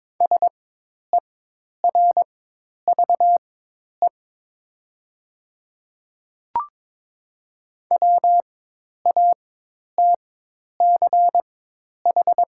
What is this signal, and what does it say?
Morse code, heard as a keyed tone, about 22 wpm, spaced as for 10 wpm, 700 Hz. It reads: SERVE WATCH
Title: 4x Character Spacing - Top 500 Words - 22wpm